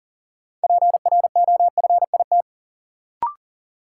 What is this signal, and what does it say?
Morse code, heard as a keyed tone, about 40 wpm, 700 Hz. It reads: PROFIT
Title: Sentences from Top 800 Words - New Word Drill - 40wpm